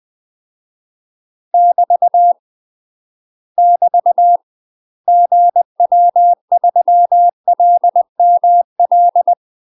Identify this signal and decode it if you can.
Morse code, heard as a keyed tone, about 20 wpm, 700 Hz. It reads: <BT> <BT> GW3LML
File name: UK MASTER QSOs TRAINING / 20WPM / UK MASTER QSOs 0011